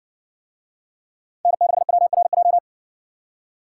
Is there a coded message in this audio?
Yes: Morse code from 1.4 to 2.6 s, heard as a keyed tone, about 60 wpm, 700 Hz. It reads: N6WRJ